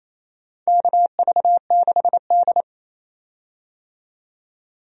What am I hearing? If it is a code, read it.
Morse code, heard as a keyed tone, about 28 wpm, 700 Hz. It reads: KV6D
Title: USA Call Signs - 4 Characters - 28wpm